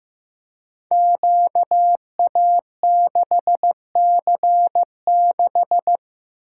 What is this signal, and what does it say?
Morse code, heard as a keyed tone, about 15 wpm, 700 Hz. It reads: QA6C6
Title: Code Groups - 5 Letter:Numbers - 15wpm